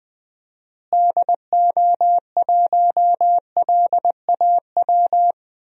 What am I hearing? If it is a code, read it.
Morse code, heard as a keyed tone, about 20 wpm, 700 Hz. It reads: DO1LAW
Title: DXCC-100 Call Signs - 6 Characters - Repeated 2x - 20wpm